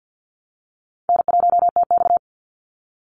Morse code, heard as a keyed tone, about 50 wpm, 700 Hz. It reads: N1TX
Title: Easy Call Signs - 50wpm